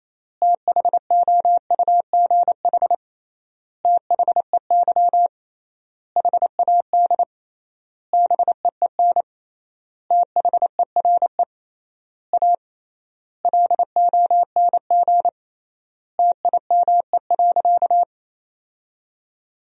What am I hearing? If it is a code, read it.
Morse code, heard as a keyed tone, about 28 wpm, 700 Hz. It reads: THOUGH THEY HAD BEEN THERE A LONG TIME.
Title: 3x Word-Spacing - Sentences from Top 400 Words - 28wpm